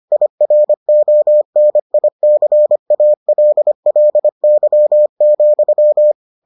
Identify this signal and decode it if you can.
Morse code, heard as a keyed tone, about 25 wpm, 600 Hz. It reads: IRONICALLY,